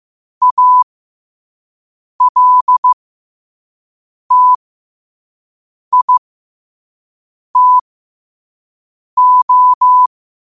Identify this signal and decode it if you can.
Morse code, heard as a keyed tone, about 15 wpm, spaced as for 5 wpm, 1000 Hz. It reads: ALTITO
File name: qotdfiles / 5wpm